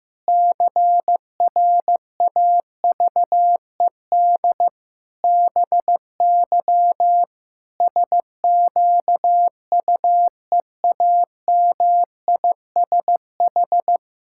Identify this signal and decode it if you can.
Morse code, heard as a keyed tone, about 15 wpm, 700 Hz. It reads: CRAVED BY SQUEAMISH
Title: Pangram Challenge v2 - 15wpm